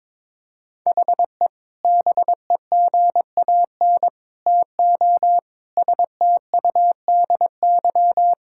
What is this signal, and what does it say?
Morse code, heard as a keyed tone, about 22 wpm, 700 Hz. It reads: HE BEGAN TO STUDY
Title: Sentences from Top 1000 Words - Review Part 6 - 22wpm